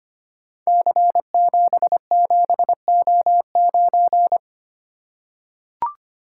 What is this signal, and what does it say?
Morse code, heard as a keyed tone, about 25 wpm, 690 Hz. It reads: C77O9
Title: Code Groups - 5 Letter:Numbers - 25wpm